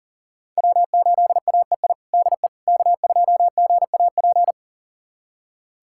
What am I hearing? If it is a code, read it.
Morse code, heard as a keyed tone, about 40 wpm, 700 Hz. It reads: W8AEI DE K2GAP
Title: USA Callsign DE Callsign - 5 Characters - 40wpm